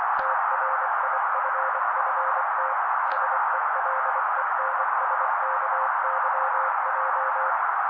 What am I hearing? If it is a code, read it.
Morse code, heard as a keyed tone, about 24 wpm, 530 Hz. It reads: TRIFFT SELENSKYJ
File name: daily / 23wpm noise